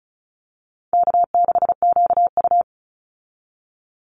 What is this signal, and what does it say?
Morse code, heard as a keyed tone, about 34 wpm, 700 Hz. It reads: K6QU
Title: Easy Call Signs Repeated 3x - 35wpm